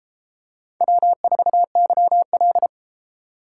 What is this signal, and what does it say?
Morse code, heard as a keyed tone, about 29 wpm, 700 Hz. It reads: W4YL